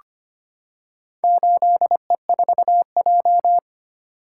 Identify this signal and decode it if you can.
Morse code, heard as a keyed tone, about 25 wpm, 700 Hz. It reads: 8E4J
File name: practice / 25wpm / Four Letter-Number Combinations - 25wpm